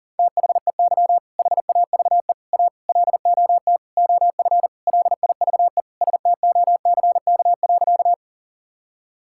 Morse code, heard as a keyed tone, about 40 wpm, 700 Hz. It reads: THEY HAVE A LOT OF LIVE STOCK.
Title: Sentences from Top 1000 Words - Review Part 5 - 40wpm